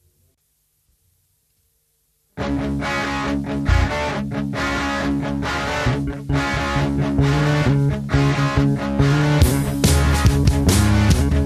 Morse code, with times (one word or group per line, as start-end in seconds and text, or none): none